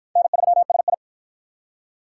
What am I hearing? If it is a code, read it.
Morse code, heard as a keyed tone, about 46 wpm, 700 Hz. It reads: N3SI